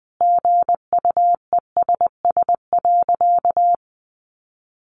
Morse code, heard as a keyed tone, about 20 wpm, 700 Hz. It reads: GUESS.
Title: Sentences from Top 1500 Words 20wpm